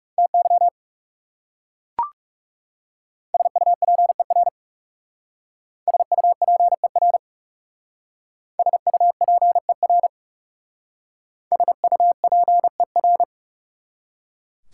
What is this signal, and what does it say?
Morse code, heard as a keyed tone, about 38 wpm, 700 Hz. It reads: TY SUPER SUPER SUPER SUPER